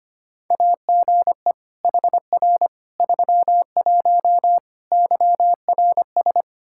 Morse code, heard as a keyed tone, about 25 wpm, 700 Hz. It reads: AGE HR 31 YRS
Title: Practice QSO - Exchange 3 - More Info - 25wpm